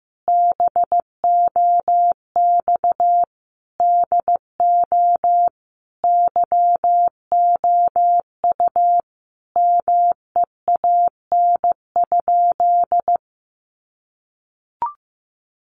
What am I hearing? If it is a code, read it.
Morse code, heard as a keyed tone, about 15 wpm, 700 Hz. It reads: BOX DO YOU MEAN?
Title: Sentences from Top 500 Words 15wpm